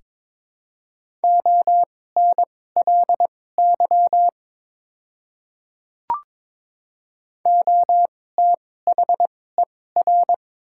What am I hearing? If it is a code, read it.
Morse code, heard as a keyed tone, about 22 wpm, spaced as for 16 wpm, 700 Hz. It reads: ONLY OTHER
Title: 2x Character Spacing - Top 200 Words - 22wpm